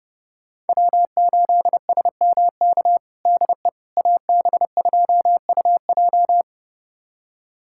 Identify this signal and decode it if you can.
Morse code, heard as a keyed tone, about 32 wpm, 700 Hz. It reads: W8SMK DE AB2UJ
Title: USA Callsign DE Callsign - 5 Characters - 30wpm